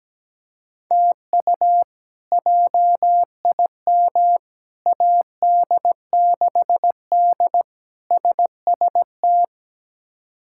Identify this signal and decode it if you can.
Morse code, heard as a keyed tone, about 17 wpm, 700 Hz. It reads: TU JIM AD6D SST
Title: K1USN SST QSO Practice - 17wpm